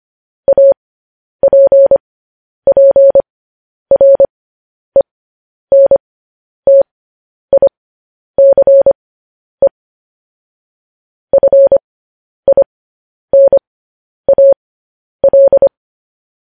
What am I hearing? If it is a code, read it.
Morse code, heard as a keyed tone, about 25 wpm, spaced as for 10 wpm, 550 Hz. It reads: APPRENTICE FINAL
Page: news headlines in morse code at 10 wpm